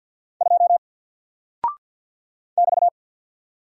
Morse code, heard as a keyed tone, about 50 wpm, 700 Hz. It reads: 2 <BT>
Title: Single Character (Letter:Number:Punctuation:Prosign) 50wpm